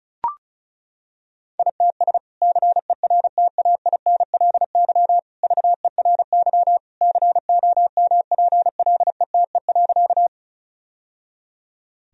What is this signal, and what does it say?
Morse code, heard as a keyed tone, about 36 wpm, 700 Hz. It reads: ITS CERTAINLY VERY COMPLETE.
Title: Sentences from Top 1000 Words - Review Part 2 - 35wpm